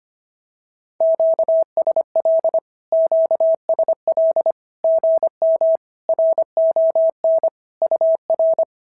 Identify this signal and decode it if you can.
Morse code, heard as a keyed tone, about 25 wpm, 690 Hz. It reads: QSL QSL GM RON UR
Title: SOTA QSO Practice 25wpm